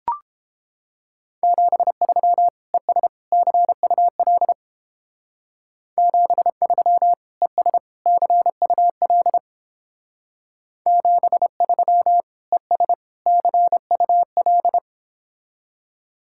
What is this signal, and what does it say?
Morse code, heard as a keyed tone, about 29 wpm, 700 Hz. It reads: 73 ES CUL 73 ES CUL 73 ES CUL